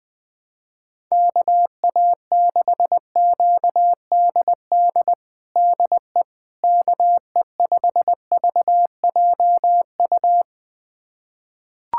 Morse code, heard as a keyed tone, about 20 wpm, 700 Hz. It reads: KA6QDD DE KE5VJU
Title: USA Callsign DE Callsign 20wpm